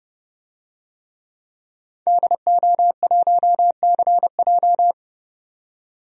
Morse code, heard as a keyed tone, about 30 wpm, 700 Hz. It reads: DO1CJ
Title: DXCC-100 Call Signs - 5 Characters - 30wpm